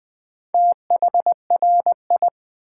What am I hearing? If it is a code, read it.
Morse code, heard as a keyed tone, about 20 wpm, 700 Hz. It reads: THRI
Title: Build A Word - 2 Letters At A Time - 8-Letter Words - 20wpm